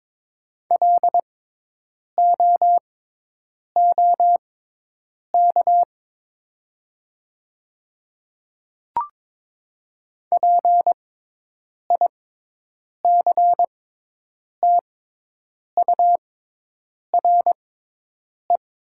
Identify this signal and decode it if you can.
Morse code, heard as a keyed tone, about 22 wpm, spaced as for 8 wpm, 700 Hz. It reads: LOOK PICTURE
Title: 6x Character Spacing - Top 200 Words - 22wpm